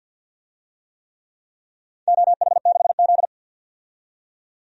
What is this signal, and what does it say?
Morse code, heard as a keyed tone, about 51 wpm, 700 Hz. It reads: OH6Z
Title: DXCC-100 Call Signs - 4 Characters - 50wpm